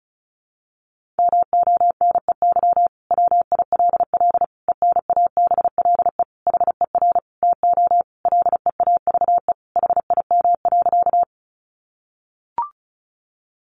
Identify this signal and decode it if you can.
Morse code, heard as a keyed tone, about 34 wpm, 700 Hz. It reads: MONEY WILL ENABLE HER TO LEAVE HIM.
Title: Sentences from Top 1500 Words 35wpm